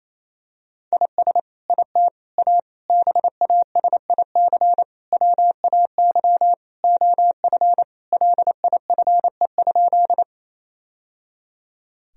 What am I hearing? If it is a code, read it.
Morse code, heard as a keyed tone, about 28 wpm, 700 Hz. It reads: IS IT A BASIC WAY OF LIFE?
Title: Sentences from Top 1000 Words - Review Part 3 - 28wpm